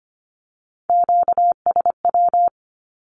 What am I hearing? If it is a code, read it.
Morse code, heard as a keyed tone, about 25 wpm, 700 Hz. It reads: QSW
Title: Q-Codes with Spoken Meaning - 25wpm